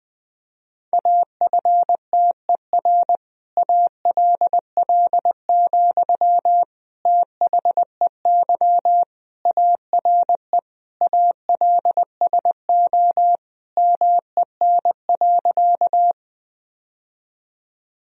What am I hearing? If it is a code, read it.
Morse code, heard as a keyed tone, about 20 wpm, 700 Hz. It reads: AFTER ALL, THEY ARE ALSO MEN.
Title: Sentences from Top 1000 Words - Review Part 1 - 20wpm